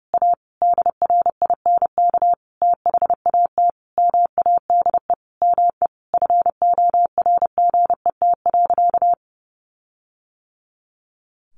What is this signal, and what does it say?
Morse code, heard as a keyed tone, about 30 wpm, 700 Hz. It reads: A DRINK THAT MADE ME FORGET.
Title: Sentences from Top 1500 Words 30wpm